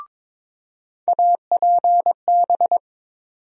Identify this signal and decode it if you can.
Morse code, heard as a keyed tone, about 22 wpm, 700 Hz. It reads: APB
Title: USA Call Sign Postfixes - 3 Letters 22wpm